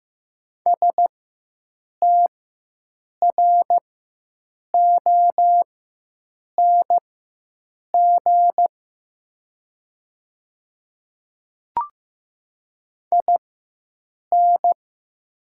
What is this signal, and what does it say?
Morse code, heard as a keyed tone, about 15 wpm, spaced as for 7 wpm, 700 Hz. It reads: STRONG IN
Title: 4x Character Spacing - Top 500 Words - 15wpm